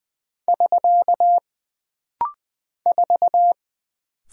Morse code, heard as a keyed tone, about 20 wpm, 700 Hz. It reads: <SK> 4
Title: Single Character (Letter:Number:Punctuation:Prosign) - Rapid-Fire - 20wpm